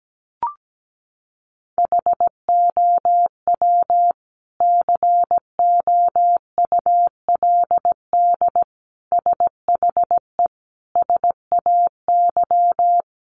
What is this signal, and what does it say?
Morse code, heard as a keyed tone, about 17 wpm, 700 Hz. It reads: HOW COULD SHE SAY
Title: Sentences from Top 200 Words 17wpm